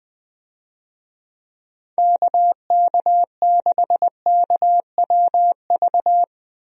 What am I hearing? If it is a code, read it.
Morse code, heard as a keyed tone, about 21 wpm, 700 Hz. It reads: KK6KWV
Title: USA Call Signs - 6 Characters - Repeated 3x - 20wpm